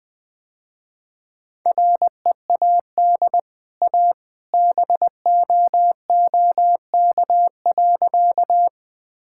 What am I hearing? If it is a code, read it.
Morse code, heard as a keyed tone, about 20 wpm, 700 Hz. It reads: READ A BOOK.
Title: Easy Sentences with Hints - 20wpm